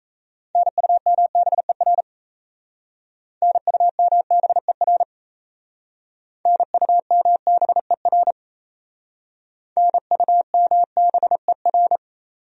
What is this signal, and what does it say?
Morse code, heard as a keyed tone, about 35 wpm, 700 Hz. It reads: NUMBER NUMBER NUMBER NUMBER